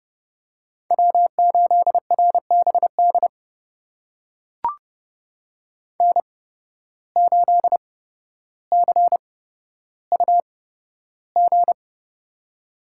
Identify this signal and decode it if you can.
Morse code, heard as a keyed tone, about 30 wpm, spaced as for 13 wpm, 700 Hz. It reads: W8RBD N8CUG
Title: USA Call Signs - 5 Characters - 8x Character Spacing - 30wpm